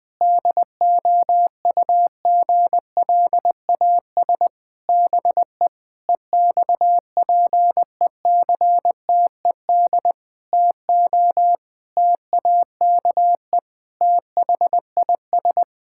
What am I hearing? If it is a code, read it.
Morse code, heard as a keyed tone, about 20 wpm, 700 Hz. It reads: DOUGLAS BE EXPECTED TO TAKE THIS
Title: Pangram Challenge v2 - 20wpm